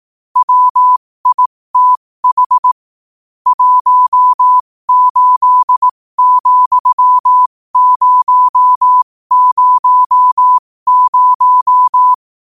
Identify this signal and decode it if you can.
Morse code, heard as a keyed tone, about 18 wpm, spaced as for 15 wpm, 1000 Hz. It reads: WITH 18,000
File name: qotdfiles / 15wpm